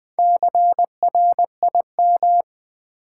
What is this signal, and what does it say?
Morse code, heard as a keyed tone, about 20 wpm, 700 Hz. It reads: CRIM
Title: Build A Word - 2 Letters At A Time - 8-Letter Words - 20wpm